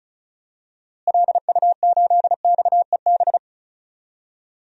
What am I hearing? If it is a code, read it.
Morse code, heard as a keyed tone, about 36 wpm, 700 Hz. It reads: LU8XEB
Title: DXCC-100 Call Signs - 6 Characters - Repeated 2x - 35wpm